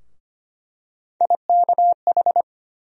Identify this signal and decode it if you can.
Morse code, heard as a keyed tone, about 25 wpm, 700 Hz. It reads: IKH
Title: USA Call Sign Postfixes - 3 Letters 25wpm